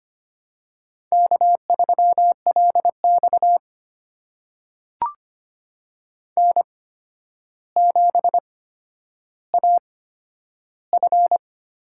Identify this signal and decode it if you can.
Morse code, heard as a keyed tone, about 25 wpm, spaced as for 11 wpm, 700 Hz. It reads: K3LX N7AF
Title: USA Call Signs - 4 Characters - 8x Character Spacing - 25wpm